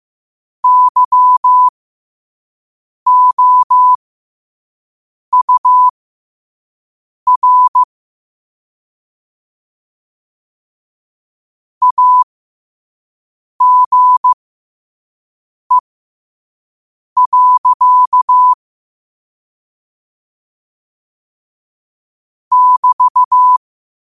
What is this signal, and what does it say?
Morse code, heard as a keyed tone, about 15 wpm, spaced as for 5 wpm, 1000 Hz. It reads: YOUR AGE. <BT>
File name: qotdfiles / 5wpm